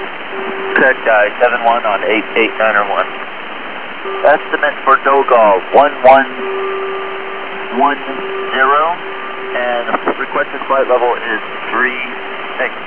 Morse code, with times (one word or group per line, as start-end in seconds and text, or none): none